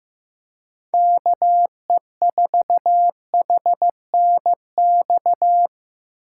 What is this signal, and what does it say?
Morse code, heard as a keyed tone, about 15 wpm, 700 Hz. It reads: KE4HNX
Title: USA Callsign DE Callsign 15wpm